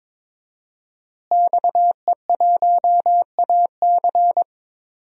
Morse code, heard as a keyed tone, about 22 wpm, 700 Hz. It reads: XE1AC
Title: DXCC-100 Call Signs - 5 Characters - Repeated 3x - 22wpm